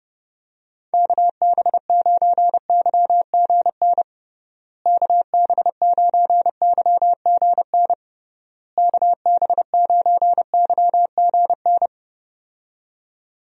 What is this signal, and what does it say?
Morse code, heard as a keyed tone, about 30 wpm, 700 Hz. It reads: KB9YGN KB9YGN KB9YGN
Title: USA Call Signs - 6 Characters - Repeated 3x - 30wpm